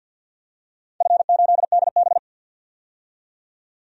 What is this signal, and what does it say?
Morse code, heard as a keyed tone, about 51 wpm, 700 Hz. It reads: F8DB
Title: DXCC-100 Call Signs - 4 Characters - 50wpm